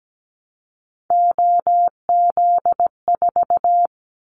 Morse code, heard as a keyed tone, about 17 wpm, 700 Hz. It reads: OZ4
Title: Callsign Prefixes for Top 100 DXCC - 17wpm